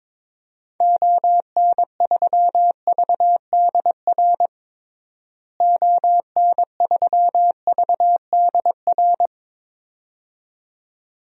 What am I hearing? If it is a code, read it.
Morse code, heard as a keyed tone, about 21 wpm, 700 Hz. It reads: ON3VDR ON3VDR
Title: DXCC-100 Call Signs - 6 Characters - Repeated 2x - 22wpm